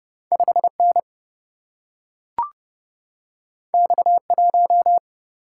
Morse code, heard as a keyed tone, about 30 wpm, 700 Hz. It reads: 5N X1
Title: Single-Letter Single-Number 30wpm